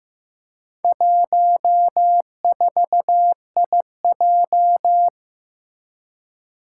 Morse code, heard as a keyed tone, about 15 wpm, 700 Hz. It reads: 14IJ
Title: Four Challenging Letter-Number Combinations 15wpm